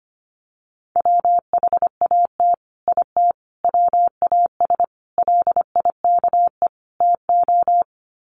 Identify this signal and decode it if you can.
Morse code, heard as a keyed tone, about 25 wpm, 700 Hz. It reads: WHAT IT WAS LIKE TO
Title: Sets of 5 Words Encore 25wpm